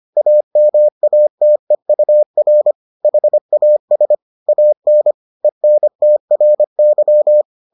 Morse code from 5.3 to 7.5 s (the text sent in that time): ENTRY